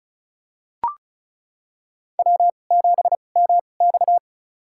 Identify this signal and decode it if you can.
Morse code, heard as a keyed tone, about 36 wpm, spaced as for 25 wpm, 700 Hz. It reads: W7MX
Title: USA Call Signs - 4 Characters - 2x Character Spacing - 35wpm